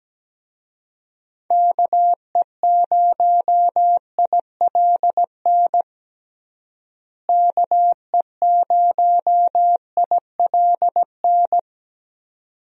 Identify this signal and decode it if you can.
Morse code, heard as a keyed tone, about 17 wpm, 700 Hz. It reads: KE0ILN KE0ILN
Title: USA Call Signs - 6 Characters - Repeated 2x - 17wpm